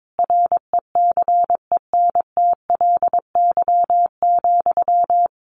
Morse code, heard as a keyed tone, about 22 wpm, 700 Hz. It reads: RECENTLY,
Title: Sentences from Top 1200 Words 22wpm